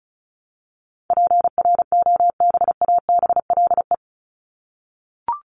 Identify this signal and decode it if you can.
Morse code, heard as a keyed tone, about 34 wpm, 700 Hz. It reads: PROBABLE
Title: LY Suffix - Top 100 Words with Base Words 35wpm